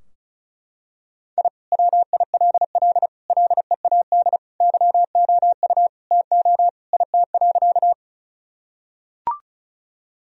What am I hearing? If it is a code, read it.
Morse code, heard as a keyed tone, about 36 wpm, 700 Hz. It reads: I WILL LEAD YOU TO IT.
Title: Sentences from Top 1000 Words - Review Part 2 - 35wpm